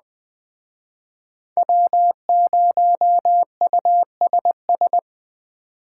Morse code, heard as a keyed tone, about 20 wpm, 700 Hz. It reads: W0USS
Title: Speed Racing - USA 1x3 Call Signs 17wpm